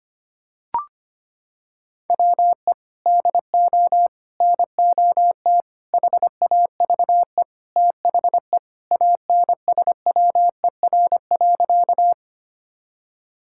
Morse code, heard as a keyed tone, about 25 wpm, 700 Hz. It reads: WE DO NOT HAVE THE ANSWER.
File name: practice / 25wpm / Sentences with 4-Word Phrases - 25wpm